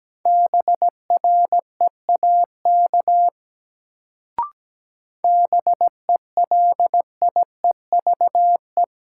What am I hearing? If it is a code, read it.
Morse code, heard as a keyed tone, about 17 wpm, 700 Hz. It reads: BREAK BELIEVE
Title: Sets of 4 Words - New Word Drill - 17wpm